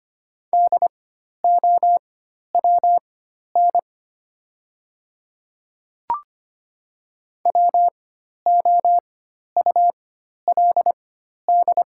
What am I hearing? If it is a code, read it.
Morse code, heard as a keyed tone, about 25 wpm, spaced as for 12 wpm, 700 Hz. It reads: DOWN WOULD
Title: 4x Character Spacing - Top 200 Words - 25wpm